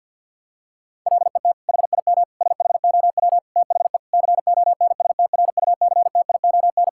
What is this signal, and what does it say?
Morse code, heard as a keyed tone, about 50 wpm, 700 Hz. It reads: LET HIM SHOW THE CONSTRUCTION
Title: Sentences from Top 1600 Words 50wpm